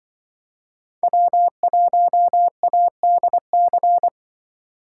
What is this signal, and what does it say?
Morse code, heard as a keyed tone, about 24 wpm, 700 Hz. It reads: W1ADC
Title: Speed Racing - USA 1x3 Call Signs 20wpm